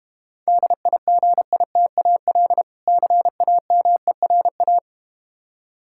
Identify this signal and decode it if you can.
Morse code, heard as a keyed tone, about 32 wpm, 700 Hz. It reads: DIGITAL CAMERA